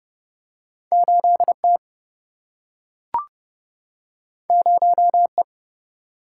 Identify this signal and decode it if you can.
Morse code, heard as a keyed tone, about 30 wpm, 700 Hz. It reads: 8T 0E
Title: Single-Letter Single-Number 30wpm